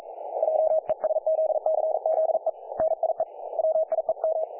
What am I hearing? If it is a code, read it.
Morse code, heard as a keyed tone, about 44 wpm, 630 Hz, with radio noise. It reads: JESZCZE NIE WIEM
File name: dlazaawansowanych / nagraniazpasma / websdr recording 3564.7kHz